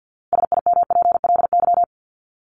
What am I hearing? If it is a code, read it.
Morse code, heard as a keyed tone, about 50 wpm, 700 Hz. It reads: SIMPLY
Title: Top 501-750 Words 50wpm